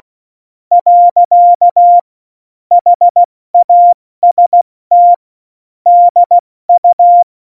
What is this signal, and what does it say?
Morse code, heard as a keyed tone, about 16 wpm, spaced as for 14 wpm, 700 Hz. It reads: . HAST DU
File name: Morsekurs / Witze / Witz 11. September 2018, 70 F80